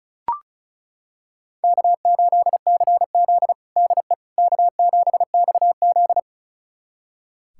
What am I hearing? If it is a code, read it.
Morse code, heard as a keyed tone, about 36 wpm, 700 Hz. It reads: K8CZ DE K7XZ
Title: USA Callsign DE Callsign - 4 Characters - 35wpm